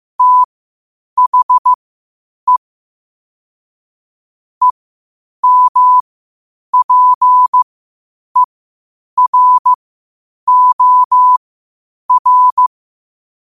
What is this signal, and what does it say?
Morse code, heard as a keyed tone, about 15 wpm, spaced as for 8 wpm, 1000 Hz. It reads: THE EMPEROR
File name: qotdfiles / 8wpm